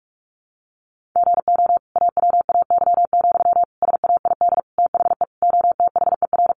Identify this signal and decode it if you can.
Morse code, heard as a keyed tone, about 44 wpm, 700 Hz. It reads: GO AWAY, SAID THE OTHER
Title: Sentences from Top 1500 Words 45wpm